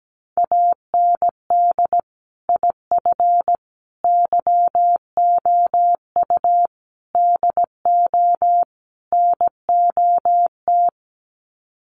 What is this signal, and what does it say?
Morse code, heard as a keyed tone, about 17 wpm, 700 Hz. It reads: AND IF YOU DO NOT
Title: Sets of 5 Words 17wpm